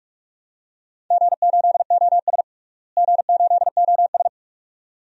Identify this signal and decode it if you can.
Morse code, heard as a keyed tone, about 46 wpm, 700 Hz. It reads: G8OS G8OS
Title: DXCC-100 Call Signs - 4 Characters - Repeated 2x - 45wpm